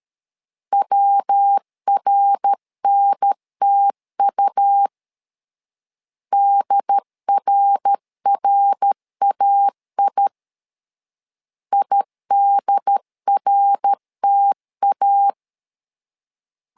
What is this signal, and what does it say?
Morse code, heard as a keyed tone, about 13 wpm, 800 Hz. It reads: WRNTU DRRAI IDRTA
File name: LEZIONI / 3 MWRO / 3 MWR 12wpm 1711